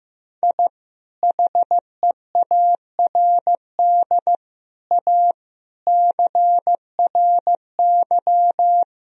Morse code, heard as a keyed tone, about 15 wpm, 700 Hz. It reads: I HEARD A CRY
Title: Sentences from Top 1000 Words - Review Part 2 - 15wpm